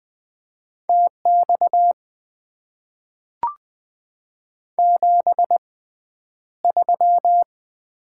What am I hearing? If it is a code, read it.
Morse code, heard as a keyed tone, about 20 wpm, spaced as for 10 wpm, 700 Hz. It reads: TX 73
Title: Top 90 QSO Elements - 6x Character Spacing - 20wpm